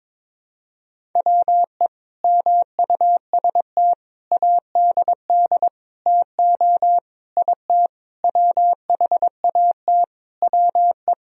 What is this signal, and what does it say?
Morse code, heard as a keyed tone, about 22 wpm, 700 Hz. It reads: WE MUST ADD TO IT WHAT WE
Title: Sentences from Top 1000 Words - Review Part 1 - 22wpm